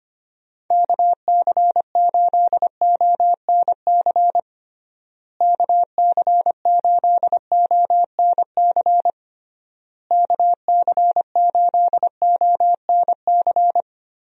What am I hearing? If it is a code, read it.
Morse code, heard as a keyed tone, about 25 wpm, 700 Hz. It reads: KC8ONC KC8ONC KC8ONC
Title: USA Call Signs - 6 Characters - Repeated 3x - 25wpm